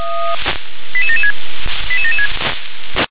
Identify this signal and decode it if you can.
Five-tone selective call (ZVEI), heard as quick runs of five tones, tones of about 70 ms. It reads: 88776 88776
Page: /357/SDR/ZVEI/88795/2023.05.25/